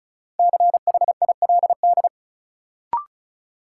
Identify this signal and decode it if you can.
Morse code, heard as a keyed tone, about 36 wpm, 700 Hz. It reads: CHILD
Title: Sentences from Top 800 Words - New Word Drill - 35wpm